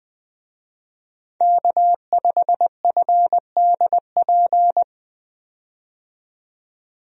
Morse code, heard as a keyed tone, about 20 wpm, 700 Hz. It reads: K5FDP
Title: USA Call Signs - 5 Characters - 20wpm